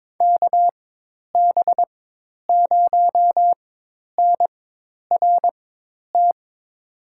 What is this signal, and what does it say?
Morse code, heard as a keyed tone, about 22 wpm, spaced as for 10 wpm, 700 Hz. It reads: KB0NRT
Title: USA Call Signs - 6 Characters - 4x Character Spacing - 22wpm